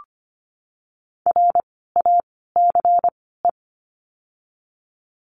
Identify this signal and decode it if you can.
Morse code, heard as a keyed tone, about 25 wpm, spaced as for 16 wpm, 700 Hz. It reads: RACE
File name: course / 25:10wpm / 094 Lesson 16f - Letter c - Words 25wpm